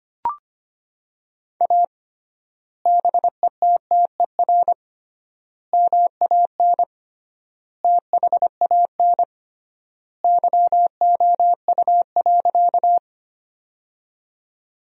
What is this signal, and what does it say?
Morse code, heard as a keyed tone, about 25 wpm, 700 Hz. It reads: A BETTER MAN THAN YOU.
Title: 3x Word-Spacing - Sentences from Top 400 Words - 25wpm